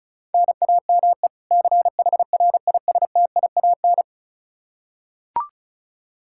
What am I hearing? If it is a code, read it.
Morse code, heard as a keyed tone, about 36 wpm, 700 Hz. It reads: NAME CHRISTIAN
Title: Practice QSO - Exchange 1 - Beginning - 35wpm